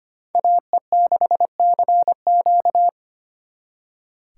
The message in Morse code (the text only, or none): AE6CQ